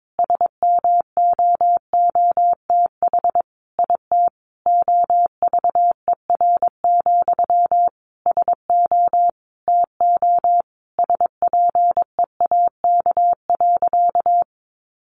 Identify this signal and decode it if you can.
Morse code, heard as a keyed tone, about 22 wpm, 700 Hz. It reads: SMOOTH IT OVER, SO TO SPEAK.
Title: Sentences from Top 1400 Words 22wpm